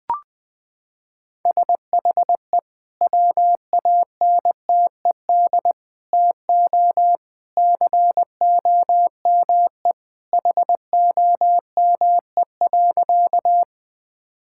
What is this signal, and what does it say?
Morse code, heard as a keyed tone, about 20 wpm, 700 Hz. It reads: SHE WANTED TO COME HOME.
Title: Sentences with 2-Word Phrases - 20wpm